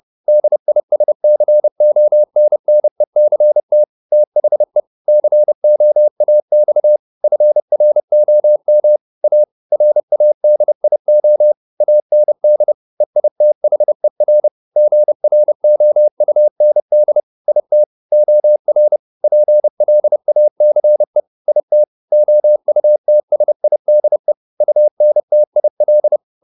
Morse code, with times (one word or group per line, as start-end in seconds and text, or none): 0.3-3.8 DISCONNECT
4.1-4.8 THE
5.1-7.0 COAX
7.2-9.0 FROM
9.2-9.4 A
9.7-11.5 RADIO
11.8-12.7 AND
13.0-14.5 EITHER
14.8-17.2 GROUND
17.5-17.8 IT
18.1-19.0 OR
19.2-21.2 PLACE
21.5-21.8 IT
22.1-24.3 OUTSIDE
24.6-26.2 UNTIL